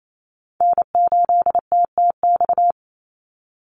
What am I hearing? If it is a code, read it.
Morse code, heard as a keyed tone, about 28 wpm, 700 Hz. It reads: N8TTX
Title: USA Call Signs 28wpm